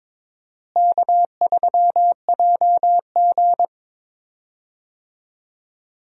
Morse code, heard as a keyed tone, about 22 wpm, 700 Hz. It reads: K3JG
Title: USA Call Signs - 4 Characters - Repeated 3x - 22wpm